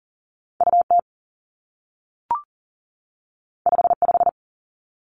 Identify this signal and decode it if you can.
Morse code, heard as a keyed tone, about 39 wpm, 700 Hz. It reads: UT 55
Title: Two Challenging Letter-Number Combinations 40wpm